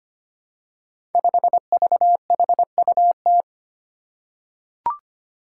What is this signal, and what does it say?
Morse code, heard as a keyed tone, about 25 wpm, 700 Hz. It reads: 5VHUT
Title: Five Challenging Letter-Number Combinations 25wpm